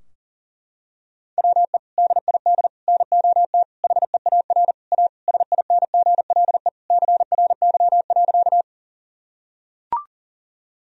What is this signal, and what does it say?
Morse code, heard as a keyed tone, about 40 wpm, 700 Hz. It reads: WE DID NOT HEAR A SINGLE CRY.
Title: Sentences from Top 1000 Words - Review Part 4 - 40wpm